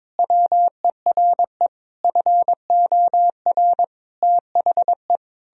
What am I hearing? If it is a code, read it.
Morse code, heard as a keyed tone, about 22 wpm, 700 Hz. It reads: WERE FOR THE
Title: Sentences from Top 1000 Words - Review Part 2 - 22wpm